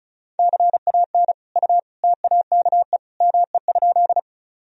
Morse code, heard as a keyed tone, about 34 wpm, 700 Hz. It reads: CAN U TAKE ME?
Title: Sentences from Top 200 Words and Abbreviations 35wpm